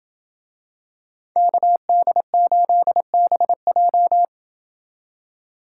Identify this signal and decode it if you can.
Morse code, heard as a keyed tone, about 26 wpm, 700 Hz. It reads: KD8BJ